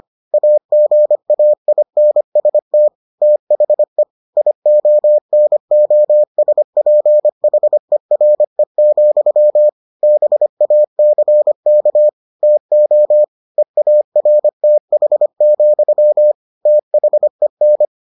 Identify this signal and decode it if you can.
Morse code, heard as a keyed tone, about 25 wpm, 600 Hz. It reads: AGAINST THE IONOSPHERE, BACK TO EARTH, THEN